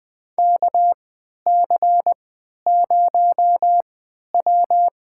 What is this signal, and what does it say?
Morse code, heard as a keyed tone, about 20 wpm, spaced as for 11 wpm, 700 Hz. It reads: KC0W
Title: USA Call Signs - 3x Character Spacing - 20wpm